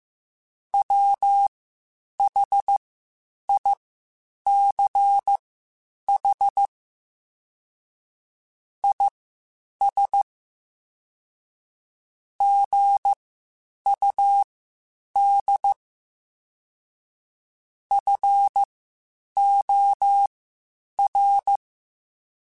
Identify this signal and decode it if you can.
Morse code, heard as a keyed tone, about 15 wpm, spaced as for 8 wpm, 800 Hz. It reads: WHICH IS GUD FOR
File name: (spaced) / QSOs Spaced-mp3s / WA7QOX 1 @13 sp